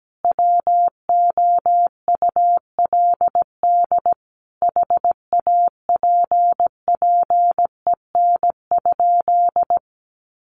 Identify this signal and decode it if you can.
Morse code, heard as a keyed tone, about 17 wpm, 700 Hz. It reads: WOULD HAPPEN?
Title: Sentences from Top 400 Words 17wpm